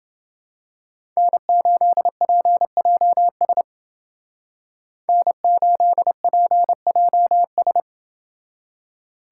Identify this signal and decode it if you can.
Morse code, heard as a keyed tone, about 30 wpm, 700 Hz. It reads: N8PJS N8PJS